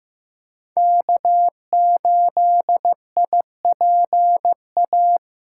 Morse code, heard as a keyed tone, about 15 wpm, 700 Hz. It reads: K8IPA
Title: USA Callsign DE Callsign - 5 Characters - 15wpm